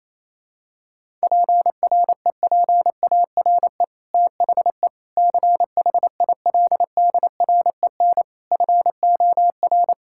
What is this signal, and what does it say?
Morse code, heard as a keyed tone, about 28 wpm, 700 Hz. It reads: PREPARE THE CHILDREN FOR
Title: Sentences from Top 1000 Words - Review Part 6 - 28wpm